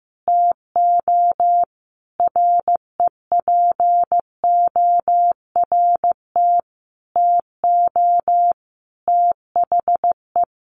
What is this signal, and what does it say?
Morse code, heard as a keyed tone, about 15 wpm, 700 Hz. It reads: TO REPORT TO THE
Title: A Sound Of Thunder 15wpm